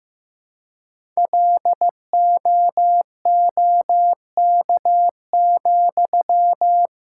Morse code, heard as a keyed tone, about 15 wpm, 700 Hz. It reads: LOOK,
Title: Sentences from Top 1000 Words - Review Part 5 - 15wpm